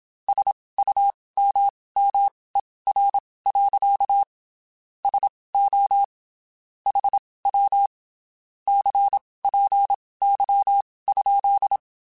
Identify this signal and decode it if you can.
Morse code, heard as a keyed tone, about 27 wpm, spaced as for 19 wpm, 800 Hz. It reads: SUMMER. SO HW CPY?
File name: (spaced) / QSOs Spaced-mp3s / WD0FBT @25 sp